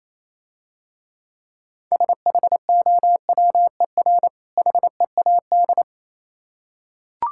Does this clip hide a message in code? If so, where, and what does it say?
Yes: Morse code from 1.9 to 5.8 s, heard as a keyed tone, about 28 wpm, 700 Hz. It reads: SHOWER HEAD